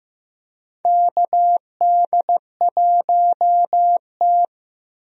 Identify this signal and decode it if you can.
Morse code, heard as a keyed tone, about 15 wpm, 700 Hz. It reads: KD1T
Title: USA Call Signs - 4 Characters - 15wpm